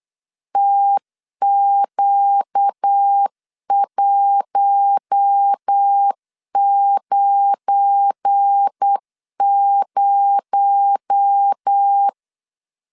Morse code, heard as a keyed tone, about 8 wpm, 800 Hz. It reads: TQ190
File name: LEZIONI / 7 GF9Z / 7 GF9 8wpm 3545